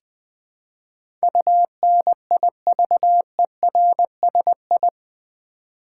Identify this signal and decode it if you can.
Morse code, heard as a keyed tone, about 20 wpm, 700 Hz. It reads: UNIVERSI
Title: Build A Word - 4 Letters At A Time - 10-Letter Words - 20wpm